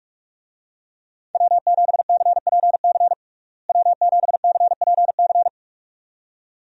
Morse code, heard as a keyed tone, about 46 wpm, 700 Hz. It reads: W7CPC W7CPC
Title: USA Call Signs - 5 Characters - Repeated 2x - 45wpm